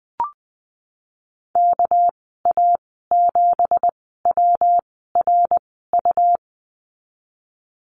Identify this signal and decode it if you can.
Morse code, heard as a keyed tone, about 20 wpm, spaced as for 14 wpm, 700 Hz. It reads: KA7WRU
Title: 245 Lesson 39g - Letter z - Call Signs 20wpm